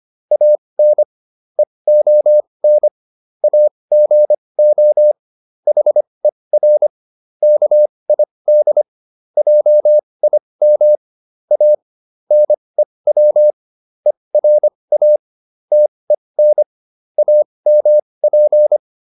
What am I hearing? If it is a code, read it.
Morse code, heard as a keyed tone, about 25 wpm, spaced as for 20 wpm, 600 Hz. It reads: AN EON AGO HER KID JIM A NEW ERA TEN AMP